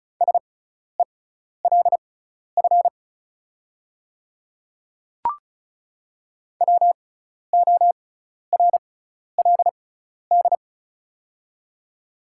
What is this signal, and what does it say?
Morse code, heard as a keyed tone, about 36 wpm, spaced as for 12 wpm, 700 Hz. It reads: SELF WORLD
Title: 6x Character Spacing - Top 500 Words - 35wpm